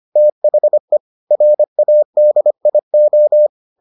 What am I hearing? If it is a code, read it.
Morse code, heard as a keyed tone, about 25 wpm, 600 Hz. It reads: THE RADIO